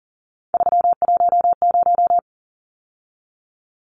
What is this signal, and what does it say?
Morse code, heard as a keyed tone, about 40 wpm, 700 Hz. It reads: 310
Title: Numbers - 3-Digit 40wpm